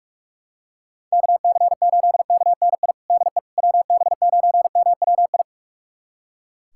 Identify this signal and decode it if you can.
Morse code, heard as a keyed tone, about 41 wpm, 690 Hz. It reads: KC8KNI DE WB9MWI
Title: USA Callsign DE Callsign 45wpm